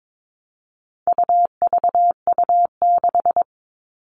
Morse code, heard as a keyed tone, about 22 wpm, 700 Hz. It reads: UVU6
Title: All Cut Numbers - 4-Digit 22wpm